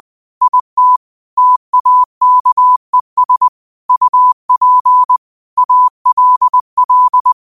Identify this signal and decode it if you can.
Morse code, heard as a keyed tone, about 20 wpm, 1000 Hz. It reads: IT TAKES UP ALL
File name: qotdfiles / 20wpm